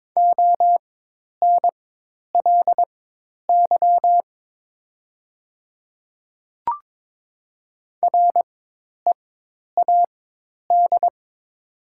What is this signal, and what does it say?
Morse code, heard as a keyed tone, about 22 wpm, spaced as for 10 wpm, 700 Hz. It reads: ONLY READ
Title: 4x Character Spacing - Top 300 Words - 22wpm